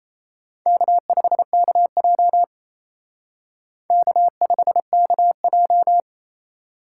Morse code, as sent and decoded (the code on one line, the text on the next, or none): -.- ..... -.- .--- / -.- ..... -.- .---
K5KJ K5KJ